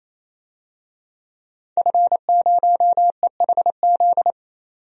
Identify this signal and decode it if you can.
Morse code, heard as a keyed tone, about 27 wpm, 700 Hz. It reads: F0EHZ
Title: DXCC-100 Call Signs - 5 Characters - 28wpm